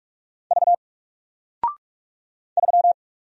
Morse code, heard as a keyed tone, about 46 wpm, 700 Hz. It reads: V 3
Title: Single Character (Letter:Number:Punctuation:Prosign) 45wpm